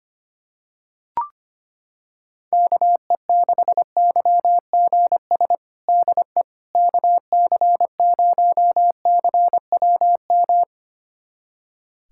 Morse code, heard as a keyed tone, about 25 wpm, 700 Hz. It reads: KE6YGS DE KC0CWM
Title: USA Callsign DE Callsign 25wpm